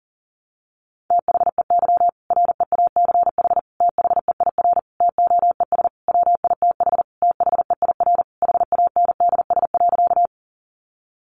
Morse code, heard as a keyed tone, about 40 wpm, 700 Hz. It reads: THEY REACH THEIR TOES WITH THEIR HANDS.
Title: Easy Phrases 40wpm